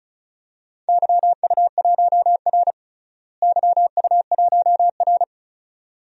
Morse code, heard as a keyed tone, about 36 wpm, 700 Hz. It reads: YU1R YU1R
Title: DXCC-100 Call Signs - 4 Characters - Repeated 2x - 35wpm